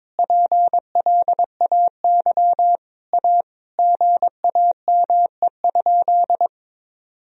Morse code, heard as a keyed tone, about 22 wpm, 700 Hz. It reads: PLAY A GAME?
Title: Sentences from Top 1000 Words - Review Part 1 - 22wpm